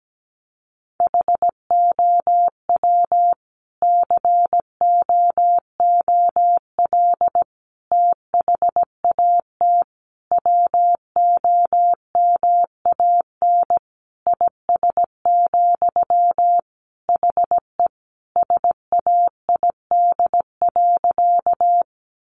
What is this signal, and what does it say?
Morse code, heard as a keyed tone, about 17 wpm, 700 Hz. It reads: HOW COOL THAT WOMAN IS, HE SAID.
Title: Sentences from Top 1200 Words 17wpm